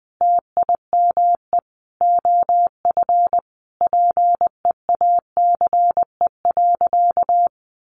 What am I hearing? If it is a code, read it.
Morse code, heard as a keyed tone, about 20 wpm, 700 Hz. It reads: TIME OF PEACE.
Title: Sentences from Top 1500 Words 20wpm